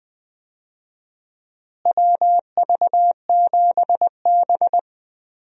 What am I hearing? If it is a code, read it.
Morse code, heard as a keyed tone, about 20 wpm, 700 Hz. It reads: WV7B
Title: USA Call Signs - 4 Characters - Repeated 2x - 20wpm